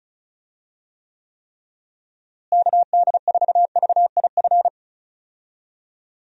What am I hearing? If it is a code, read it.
Morse code, heard as a keyed tone, about 36 wpm, 700 Hz. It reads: KD4VIF
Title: USA Call Signs - 6 Characters - Repeated 3x - 35wpm